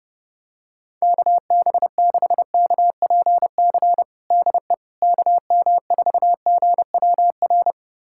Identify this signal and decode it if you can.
Morse code, heard as a keyed tone, about 31 wpm, 700 Hz. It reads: KB6KPC DE KM4GWR
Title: USA Callsign DE Callsign - 6 Characters - 30wpm